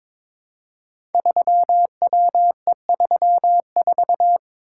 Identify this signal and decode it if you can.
Morse code, heard as a keyed tone, about 23 wpm, 700 Hz. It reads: 3WE34
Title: Code Groups - 5 Letter:Numbers - 22wpm